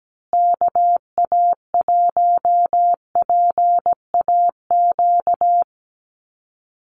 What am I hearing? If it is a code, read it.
Morse code, heard as a keyed tone, about 17 wpm, 700 Hz. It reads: KA1PAQ
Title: USA Call Signs Repeated 3x 17wpm